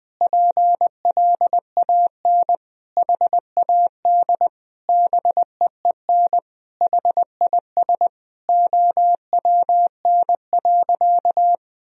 Morse code, heard as a keyed tone, about 20 wpm, 700 Hz. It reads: PLAN HAD BEEN HIS OWN.
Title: Sentences from Top 1000 Words - Review Part 2 - 20wpm